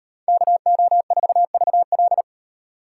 Morse code, heard as a keyed tone, about 38 wpm, 700 Hz. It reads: KO4VL